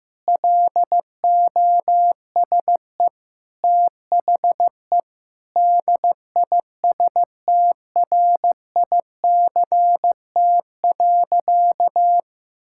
Morse code, heard as a keyed tone, about 15 wpm, 700 Hz. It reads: LOSE THE DISTRICT.
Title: Sentences from Top 1600 Words 15wpm